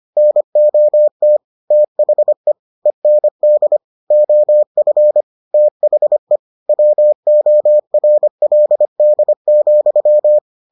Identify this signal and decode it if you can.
Morse code, heard as a keyed tone, about 25 wpm, 600 Hz. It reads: NOT THE END OF THE WORLD,